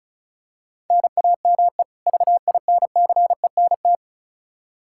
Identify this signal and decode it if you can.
Morse code, heard as a keyed tone, about 36 wpm, 700 Hz. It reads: NAME VINCENT
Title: Practice QSO - Exchange 1 - Beginning - 35wpm